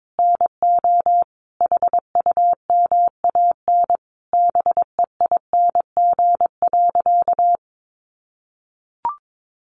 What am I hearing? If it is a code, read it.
Morse code, heard as a keyed tone, about 22 wpm, 700 Hz. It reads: NO HUMAN BEING.
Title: Sentences from Top 1500 Words 22wpm